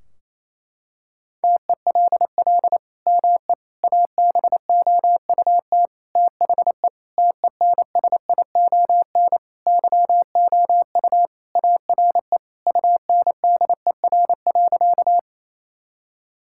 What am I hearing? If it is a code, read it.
Morse code, heard as a keyed tone, about 28 wpm, 700 Hz. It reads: TELL ME ABOUT THE TENSION YOU ARE UNDER.
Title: Sentences from Top 1600 Words 28wpm